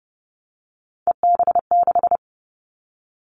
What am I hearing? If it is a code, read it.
Morse code, heard as a keyed tone, about 32 wpm, 700 Hz. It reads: EB6
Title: Callsign Prefixes for Top 50 DXCC - 30wpm